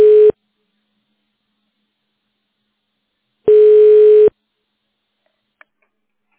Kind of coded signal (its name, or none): telephone call progress tone